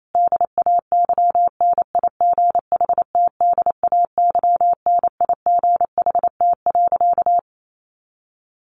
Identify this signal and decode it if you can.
Morse code, heard as a keyed tone, about 28 wpm, 700 Hz. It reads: DAYNIGHTDAYNIGHT.
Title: A Sound Of Thunder 28wpm